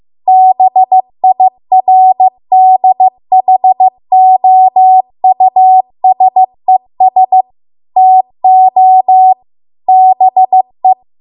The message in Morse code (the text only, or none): BIRDHOUSES TO BE